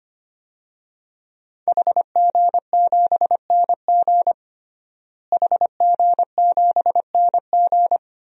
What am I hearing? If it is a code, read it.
Morse code, heard as a keyed tone, about 25 wpm, 700 Hz. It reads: HG7NG HG7NG
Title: DXCC-100 Call Signs - 5 Characters - Repeated 2x - 25wpm